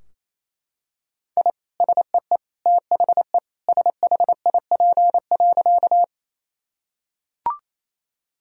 Morse code, heard as a keyed tone, about 28 wpm, 700 Hz. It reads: I SEE THE SHIP.
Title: Sentences from Top 1000 Words - Review Part 2 - 28wpm